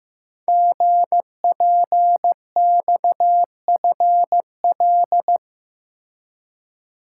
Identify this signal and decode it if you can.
Morse code, heard as a keyed tone, about 15 wpm, 700 Hz. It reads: GPXFL
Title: Code Groups - 5 Letters - 15wpm